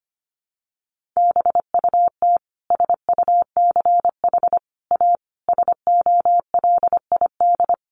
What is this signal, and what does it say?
Morse code, heard as a keyed tone, about 25 wpm, 700 Hz. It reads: BUT SUCH A SOLID
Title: Sentences from Top 1500 Words 25wpm